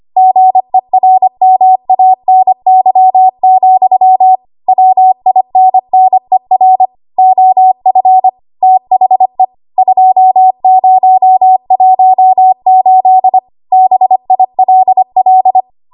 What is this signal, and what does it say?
Morse code, heard as a keyed tone, about 25 wpm, 750 Hz. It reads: GERMANY, WINNER OF THE 2018 BILL
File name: Archive / 25 WPM / 25WPM